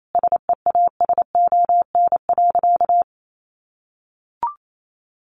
Morse code, heard as a keyed tone, about 28 wpm, 700 Hz. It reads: SEASON.
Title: Sentences from Top 1400 Words 28wpm